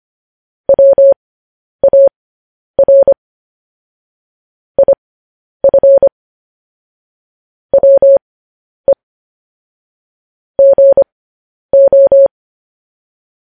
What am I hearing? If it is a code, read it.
Morse code, heard as a keyed tone, about 25 wpm, spaced as for 10 wpm, 550 Hz. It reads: WAR IF WE GO
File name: podcast / News headlines-1x-25x10